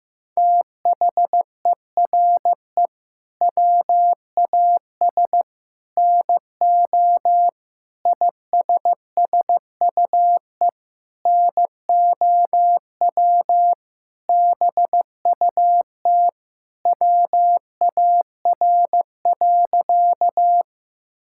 Morse code, heard as a keyed tone, about 14 wpm, 700 Hz. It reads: THERE WAS NO ISSUE NOW BUT WAR.
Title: Sentences from Top 1000 Words - Review Part 6 - 15wpm